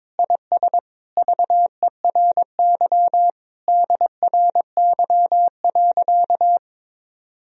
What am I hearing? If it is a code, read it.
Morse code, heard as a keyed tone, about 22 wpm, 700 Hz. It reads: IS VERY DRY.
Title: Sentences from Top 1000 Words - Review Part 4 - 22wpm